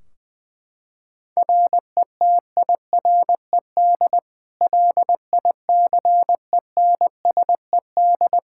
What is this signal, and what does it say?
Morse code, heard as a keyed tone, about 21 wpm, 700 Hz. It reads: RETIRED LICENSED
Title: K1USN SST QSO Practice 20wpm